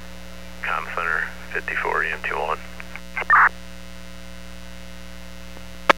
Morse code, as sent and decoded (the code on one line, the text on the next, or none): none